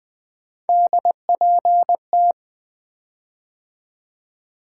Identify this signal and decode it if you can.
Morse code, heard as a keyed tone, about 20 wpm, 700 Hz. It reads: DPT
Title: Three Letter-Number Combinations - 20wpm